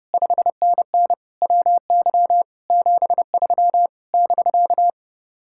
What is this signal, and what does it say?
Morse code, heard as a keyed tone, about 33 wpm, 700 Hz. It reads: 5NN WY 73 <BK>